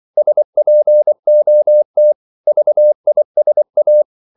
Morse code, heard as a keyed tone, about 24 wpm, 600 Hz. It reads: SPOT VISA
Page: Korte danske ord 24wpm